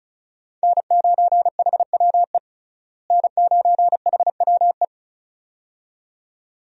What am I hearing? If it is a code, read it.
Morse code, heard as a keyed tone, about 36 wpm, 700 Hz. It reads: N9HWE N9HWE
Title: USA Call Signs - 5 Characters - Repeated 2x - 35wpm